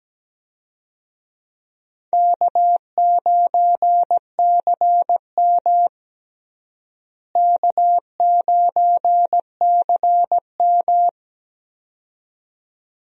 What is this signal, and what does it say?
Morse code, heard as a keyed tone, about 17 wpm, 700 Hz. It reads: K9CM K9CM
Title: USA Call Signs - 4 Characters - Repeated 2x - 17wpm